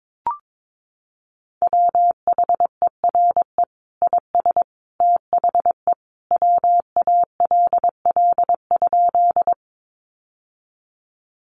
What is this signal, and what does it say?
Morse code, heard as a keyed tone, about 22 wpm, 700 Hz. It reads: WHERE IS THE WALL?
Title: Sentences from Top 1100 Words 22wpm